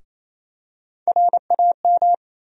Morse code, heard as a keyed tone, about 28 wpm, 700 Hz. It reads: RAM
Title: USA Call Sign Postfixes - 3 Letters 28wpm